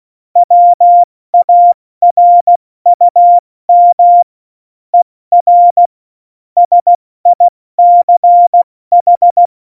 Morse code, heard as a keyed tone, about 16 wpm, 700 Hz. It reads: WARUM ER SICH
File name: Morsekurs / Witze / Witz 10.Dezember 2019, 70 F80